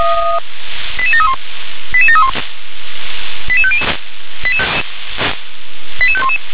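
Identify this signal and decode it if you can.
Five-tone selective call (ZVEI), heard as quick runs of five tones, tones of about 70 ms. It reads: 88731 88731 88500 88511
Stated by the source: ZVEI five-tone selective call 88730 received by SDR.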